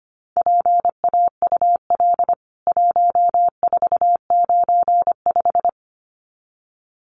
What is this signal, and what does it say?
Morse code, heard as a keyed tone, about 25 wpm, 700 Hz. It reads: PAUL 1495
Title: CWT Exchange 25wpm